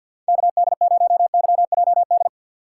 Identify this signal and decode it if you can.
Morse code, heard as a keyed tone, about 48 wpm, 700 Hz. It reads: KD0YJD
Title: USA Call Signs - 6 Characters - 6x Character Spacing - 50wpm